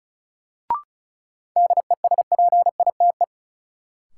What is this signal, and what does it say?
Morse code, heard as a keyed tone, about 34 wpm, 700 Hz. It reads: DESPITE
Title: Sets of 4 Words - New Word Drill - 35wpm